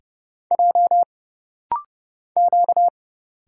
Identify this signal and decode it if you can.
Morse code, heard as a keyed tone, about 30 wpm, 700 Hz. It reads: J Q
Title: Single Letters - Rapid-Fire 30wpm